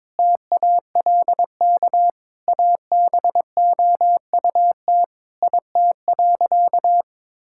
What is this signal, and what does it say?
Morse code, heard as a keyed tone, about 22 wpm, 700 Hz. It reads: TALK ABOUT IT.
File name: practice / 22wpm / Sentences with 2-Word Phrases - 22wpm